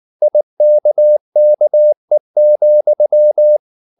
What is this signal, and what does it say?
Morse code, heard as a keyed tone, about 19 wpm, 600 Hz. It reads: IKKE,
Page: Ferie 19 wpm